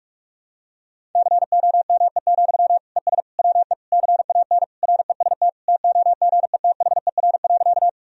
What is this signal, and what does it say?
Morse code, heard as a keyed tone, about 46 wpm, 690 Hz. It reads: COME, ES WE CAN REST TOGETHER.
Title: Sentences from Top 500 Words and Abbreviations 45wpm